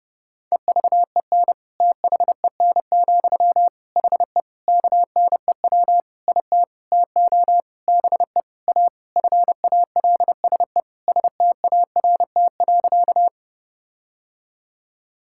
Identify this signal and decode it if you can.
Morse code, heard as a keyed tone, about 30 wpm, 700 Hz. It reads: EVEN THEN, HE KNEW IT TO BE A FALSE START.
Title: Sentences from Top 1600 Words 30wpm